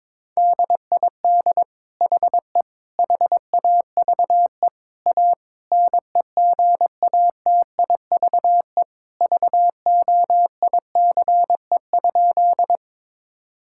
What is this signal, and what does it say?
Morse code, heard as a keyed tone, about 22 wpm, 700 Hz. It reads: DID HE HAVE A NEGATIVE VOICE?
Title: Sentences from Top 1000 Words - Review Part 6 - 22wpm